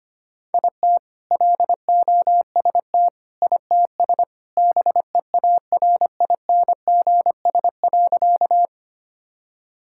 Morse code, heard as a keyed tone, about 25 wpm, 700 Hz. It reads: IT LOST ITS BEARINGS.
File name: practice / 25wpm / All-Age Appropriate Jokes - 25wpm